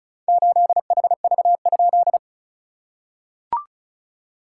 Morse code, heard as a keyed tone, about 36 wpm, 690 Hz. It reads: 8HV?
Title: SOTA QSO Practice 35wpm